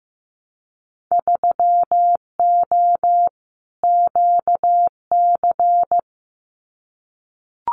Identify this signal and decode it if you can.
Morse code, heard as a keyed tone, about 15 wpm, 700 Hz. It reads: 3O QC
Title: Winter Field Day Exchanges Encore 15wpm